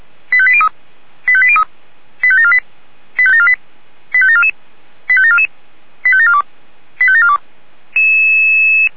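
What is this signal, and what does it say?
Five-tone selective call (ZVEI), heard as quick runs of five tones, tones of about 70 ms. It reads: 85803 85803 85748 85748 85744 85744 85742 85742
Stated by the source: ZVEI five-tone selective call 85719 received by SDR.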